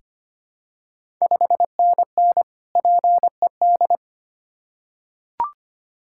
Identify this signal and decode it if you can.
Morse code, heard as a keyed tone, about 25 wpm, 700 Hz. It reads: 5NN PED
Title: Ontario QSO Party Exchange 25wpm